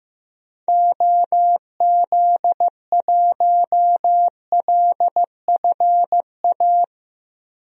Morse code, heard as a keyed tone, about 15 wpm, 700 Hz. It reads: OZ1LFA
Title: DXCC-100 Call Signs - 6 Characters - Repeated 3x - 15wpm